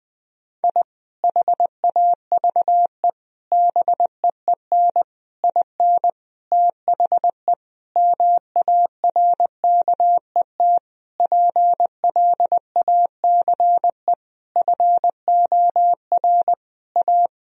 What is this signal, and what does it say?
Morse code, heard as a keyed tone, about 20 wpm, 700 Hz. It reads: I HAVE BEEN IN THE MARKET PLACE FOR A
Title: Sentences from Top 1000 Words - Review Part 3 - 20wpm